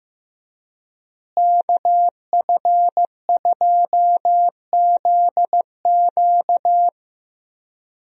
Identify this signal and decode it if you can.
Morse code, heard as a keyed tone, about 15 wpm, 700 Hz. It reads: KF2ZQ
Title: USA Call Signs - 5 Characters - Repeated 3x - 15wpm